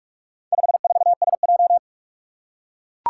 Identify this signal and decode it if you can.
Morse code, heard as a keyed tone, about 44 wpm, 700 Hz. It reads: 54SJ
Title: Four Challenging Letter-Number Combinations 45wpm